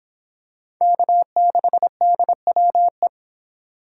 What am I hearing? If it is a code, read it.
Morse code, heard as a keyed tone, about 25 wpm, 700 Hz. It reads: K6DWE